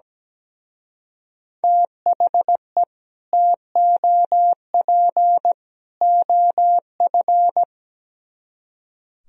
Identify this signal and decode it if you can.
Morse code, heard as a keyed tone, about 17 wpm, 700 Hz. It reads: THE TOP OF
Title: Speed-Racing - Sets of 3 Words 17wpm